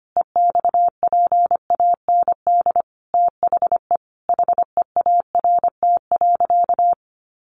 Morse code, heard as a keyed tone, about 25 wpm, 700 Hz. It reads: EXPAND THE HEART.
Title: Sentences from Top 1400 Words 25wpm